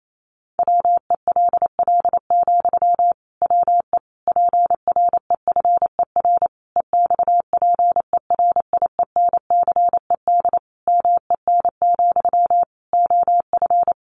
Morse code, heard as a keyed tone, about 28 wpm, 700 Hz. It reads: WELL, WE PREFER EXPERIENCED MEN, OF
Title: Sentences from Top 1100 Words 28wpm